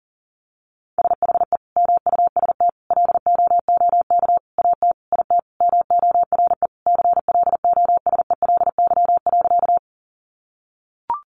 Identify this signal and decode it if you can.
Morse code, heard as a keyed tone, about 40 wpm, 700 Hz. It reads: SHE MUST LOOK AT IT MORE CLOSELY.
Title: Sentences from Top 1400 Words 40wpm